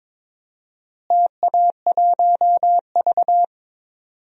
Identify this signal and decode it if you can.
Morse code, heard as a keyed tone, about 22 wpm, 700 Hz. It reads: TA1V
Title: DXCC-100 Call Signs - 4 Characters - Repeated 3x - 22wpm